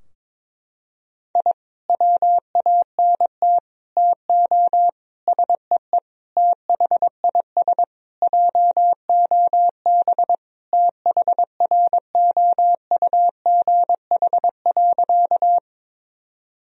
Morse code, heard as a keyed tone, about 22 wpm, 700 Hz. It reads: I WANT TO SEE THIS JOB THROUGH.
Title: Sentences from Top 1000 Words - Review Part 3 - 22wpm